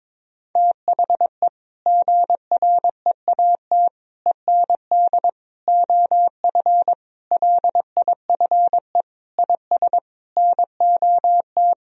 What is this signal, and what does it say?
Morse code, heard as a keyed tone, about 22 wpm, 700 Hz. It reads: THE GREAT END OF LIFE IS NOT
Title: Sentences from Top 1000 Words - Review Part 4 - 22wpm